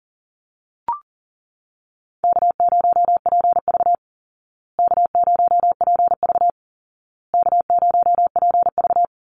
Morse code, heard as a keyed tone, about 40 wpm, 700 Hz. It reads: K0PV K0PV K0PV
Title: Call Signs Repeated 2x - 40wpm